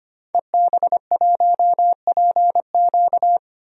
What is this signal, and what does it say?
Morse code, heard as a keyed tone, about 25 wpm, 700 Hz. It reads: EB1PQ
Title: DXCC-100 Call Signs - 5 Characters - 25wpm